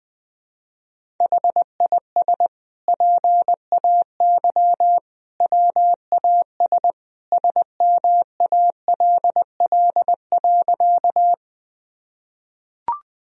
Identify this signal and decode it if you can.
Morse code, heard as a keyed tone, about 21 wpm, 700 Hz. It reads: HIS PAY WAS SMALL.
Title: Sentences from Top 1000 Words - Review Part 1 - 20wpm